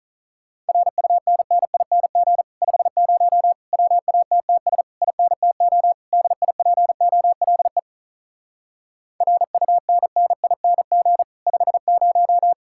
Morse code, heard as a keyed tone, about 47 wpm, 700 Hz. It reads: RUNNING 50 WATTS INTO DIPOLE RUNNING 50